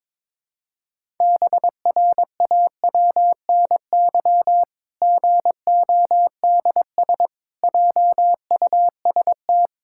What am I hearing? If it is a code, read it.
Morse code, heard as a keyed tone, about 22 wpm, 700 Hz. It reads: BRAWNY GODS JUST
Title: Pangram Challenge v2 - 22wpm